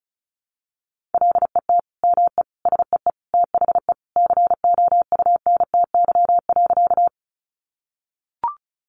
Morse code, heard as a keyed tone, about 34 wpm, 700 Hz. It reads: LET ME SEE THE COUNTY.
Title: Sentences from Top 1200 Words 35wpm